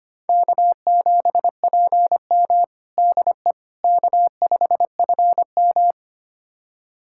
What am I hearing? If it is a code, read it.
Morse code, heard as a keyed tone, about 25 wpm, 700 Hz. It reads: K7PM DE K5FM
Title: USA Callsign DE Callsign - 4 Characters - 25wpm